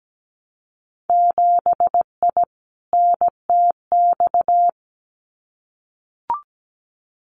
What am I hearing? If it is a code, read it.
Morse code, heard as a keyed tone, about 17 wpm, 700 Hz. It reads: 7I NTX
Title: Winter Field Day Exchanges Encore 17wpm